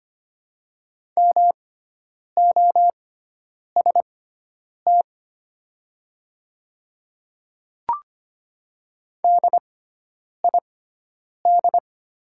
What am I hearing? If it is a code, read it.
Morse code, heard as a keyed tone, about 25 wpm, spaced as for 9 wpm, 700 Hz. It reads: MOST DID
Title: 6x Character Spacing - Top 100 Words - 25wpm